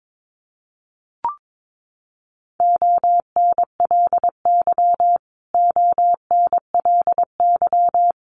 Morse code, heard as a keyed tone, about 22 wpm, 700 Hz. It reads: ONLY ONLY
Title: Top 500 Words Repeated 22wpm